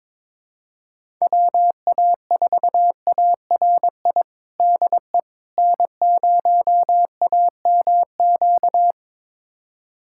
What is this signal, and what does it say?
Morse code, heard as a keyed tone, about 22 wpm, 700 Hz. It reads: WA4ARI DE N0AMQ
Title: USA Callsign DE Callsign 22wpm